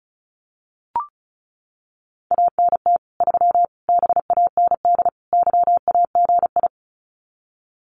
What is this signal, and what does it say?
Morse code, heard as a keyed tone, about 34 wpm, 700 Hz. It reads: ANT 3 BAND YAGI
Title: Common QSO Phrases 35wpm